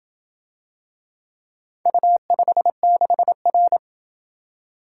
Morse code, heard as a keyed tone, about 27 wpm, 700 Hz. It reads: U56R